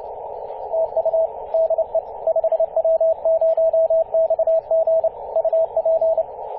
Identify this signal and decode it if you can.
Morse code, heard as a keyed tone, about 29 wpm, 650 Hz, with radio noise. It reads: TU DE 5W0XG UP
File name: 5w0xg-18-075mhz-cw